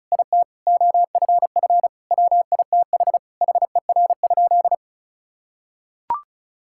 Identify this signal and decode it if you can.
Morse code, heard as a keyed tone, about 34 wpm, 700 Hz. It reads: IT OFF WITH HER?
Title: Sentences from Top 1000 Words - Review Part 6 - 35wpm